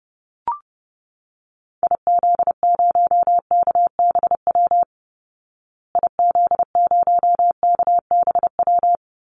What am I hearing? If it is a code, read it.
Morse code, heard as a keyed tone, about 28 wpm, 700 Hz. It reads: IZ0KBW IZ0KBW
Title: Call Signs Repeated 2x - 30wpm